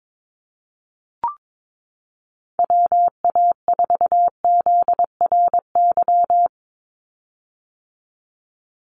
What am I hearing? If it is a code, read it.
Morse code, heard as a keyed tone, about 22 wpm, 700 Hz. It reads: WA4ZRY
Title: USA Call Signs 22wpm